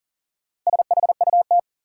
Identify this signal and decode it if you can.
Morse code, heard as a keyed tone, about 40 wpm, 700 Hz. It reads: SHUT
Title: Four Letter Words - 40wpm